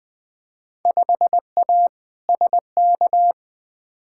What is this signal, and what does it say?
Morse code, heard as a keyed tone, about 20 wpm, 700 Hz. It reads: 5A SK
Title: Field Day Exchanges Encore - 20wpm